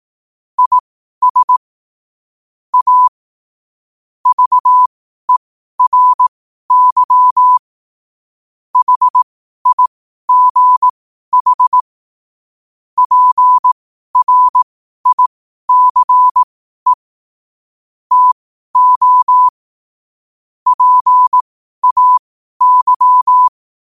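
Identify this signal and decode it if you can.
Morse code, heard as a keyed tone, about 18 wpm, spaced as for 12 wpm, 1000 Hz. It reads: IS A VERY HIGH PRICE TO PAY
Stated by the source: Morse code quotes at 12 words per minute.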